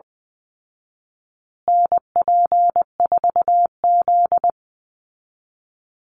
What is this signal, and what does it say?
Morse code, heard as a keyed tone, about 20 wpm, 700 Hz. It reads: NP4Z
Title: Call Signs Repeated 2x - 20wpm